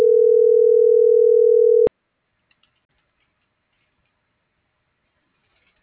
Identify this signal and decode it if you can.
Ringback tone (North America), 440 + 480 Hz.